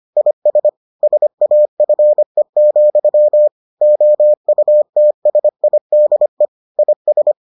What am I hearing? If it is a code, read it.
Morse code, heard as a keyed tone, about 25 wpm, 600 Hz. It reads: IS SAFE, OUTSIDE IS